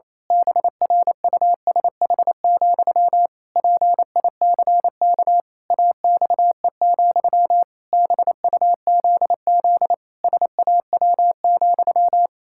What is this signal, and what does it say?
Morse code, heard as a keyed tone, about 27 wpm, 700 Hz. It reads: BRUSH, PICK AXE, BUZZ SAW,
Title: Pangram Challenge v2 - 28wpm